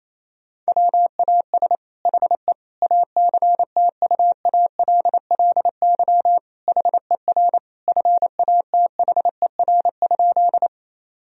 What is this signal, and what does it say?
Morse code, heard as a keyed tone, about 28 wpm, 700 Hz. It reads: WAS HE ACTUALLY HER FATHER?
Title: Sentences from Top 1000 Words - Review Part 4 - 28wpm